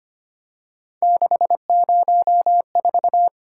Morse code, heard as a keyed tone, about 25 wpm, 700 Hz. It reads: 604
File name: Three Letter-Number Combinations - 25wpm